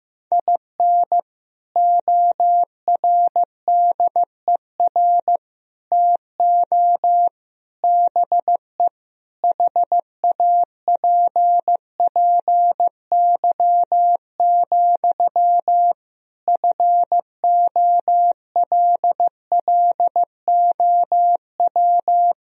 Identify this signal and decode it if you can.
Morse code, heard as a keyed tone, about 15 wpm, 700 Hz. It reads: IN ORDER TO BE HAPPY, FOLLOW
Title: Sentences from Top 1000 Words - Review Part 2 - 15wpm